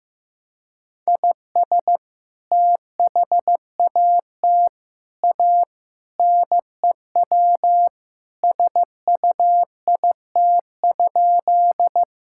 Morse code, heard as a keyed tone, about 15 wpm, 700 Hz. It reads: IS THAT A NEW SUIT?
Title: Sentences from Top 1600 Words 15wpm